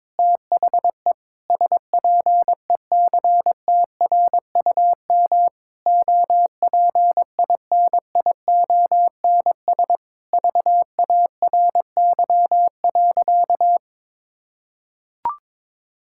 Morse code, heard as a keyed tone, about 22 wpm, 700 Hz. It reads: THE SPECTRUM OPINIONS VARY.
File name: Sentences with 7-Word Phrases - 22wpm